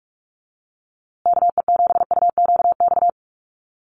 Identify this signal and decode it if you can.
Morse code, heard as a keyed tone, about 44 wpm, 700 Hz. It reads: KE7UQX
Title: USA Call Signs Repeated 2x 45wpm